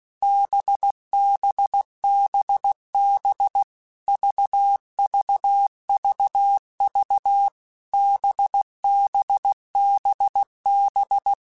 Morse code, heard as a keyed tone, about 16 wpm, 800 Hz. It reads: BBBB VVVV BBBB
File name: Basic Letter Groups By7Groups / VB46 / VB46@15